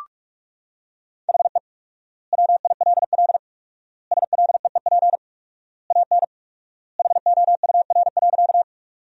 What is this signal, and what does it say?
Morse code, heard as a keyed tone, about 46 wpm, 700 Hz. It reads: HE WILL SLEEP AN HOUR.
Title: 4x Word-Spacing - Sentences from Top 500 Words - 45wpm